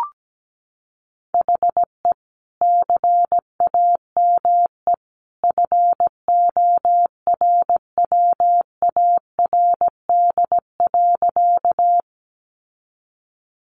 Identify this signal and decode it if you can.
Morse code, heard as a keyed tone, about 17 wpm, 700 Hz. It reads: HE CAME FORWARD.
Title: Sentences from Top 1300 Words 17wpm